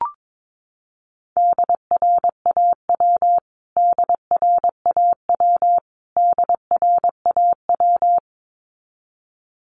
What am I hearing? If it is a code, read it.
Morse code, heard as a keyed tone, about 22 wpm, 700 Hz. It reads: DRAW DRAW DRAW
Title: Top 500 Words Repeated 22wpm